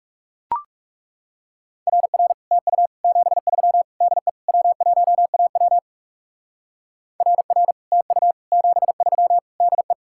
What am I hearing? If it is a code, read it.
Morse code, heard as a keyed tone, about 38 wpm, 700 Hz. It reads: RR TU 73 DE W1AW RR TU 73 DE